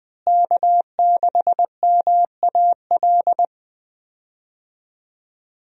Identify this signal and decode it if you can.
Morse code, heard as a keyed tone, about 20 wpm, 700 Hz. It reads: K6MAL
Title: USA Call Signs - 5 Characters - Repeated 3x - 20wpm